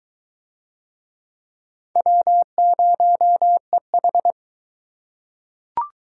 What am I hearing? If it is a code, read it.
Morse code, heard as a keyed tone, about 22 wpm, 700 Hz. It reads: W0EH